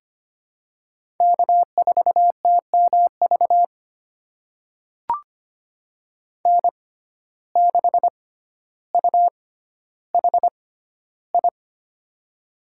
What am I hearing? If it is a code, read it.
Morse code, heard as a keyed tone, about 25 wpm, spaced as for 13 wpm, 700 Hz. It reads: K4TMV N6UHI
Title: USA Call Signs - 5 Characters - 6x Character Spacing - 25wpm